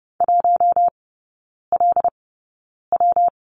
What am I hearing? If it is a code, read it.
Morse code, heard as a keyed tone, about 30 wpm, spaced as for 9 wpm, 700 Hz. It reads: 1LW
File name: Lesson 19e - Letter w - Three Characters 30wpm